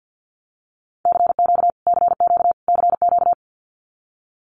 Morse code, heard as a keyed tone, about 50 wpm, 700 Hz. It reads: CQ CQ CQ
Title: Abbreviations Repeated 3x - 50wpm